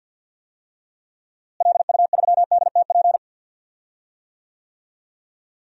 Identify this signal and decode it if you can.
Morse code, heard as a keyed tone, about 45 wpm, 700 Hz. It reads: LU3DTP
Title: DXCC-100 Call Signs - 6 Characters - Repeated 3x - 50wpm